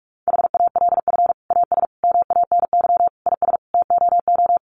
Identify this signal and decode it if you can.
Morse code, heard as a keyed tone, about 44 wpm, 700 Hz. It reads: HALF AS MANY IS TOO
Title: Sentences from Top 400 Words 45wpm